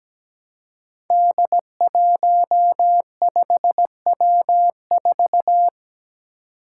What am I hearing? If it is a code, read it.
Morse code, heard as a keyed tone, about 17 wpm, 700 Hz. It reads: D15W4
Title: Five Challenging Letter-Number Combinations 17wpm